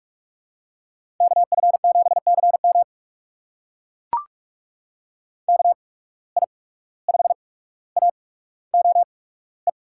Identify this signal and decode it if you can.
Morse code, heard as a keyed tone, about 44 wpm, spaced as for 18 wpm, 700 Hz. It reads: KF7CM KI5AOE
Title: USA Call Signs - 8x Character Spacing - 45wpm